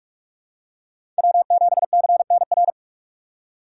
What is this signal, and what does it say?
Morse code, heard as a keyed tone, about 46 wpm, 700 Hz. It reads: W7CNR